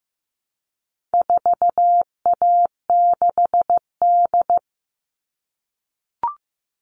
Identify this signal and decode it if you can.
Morse code, heard as a keyed tone, about 15 wpm, 700 Hz. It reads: 4A6D
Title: All Cut Numbers - 4-Digit 15wpm